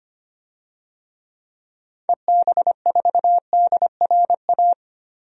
Morse code, heard as a keyed tone, about 25 wpm, 700 Hz. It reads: EB4DRA
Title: DXCC-100 Call Signs - 6 Characters - Repeated 3x - 25wpm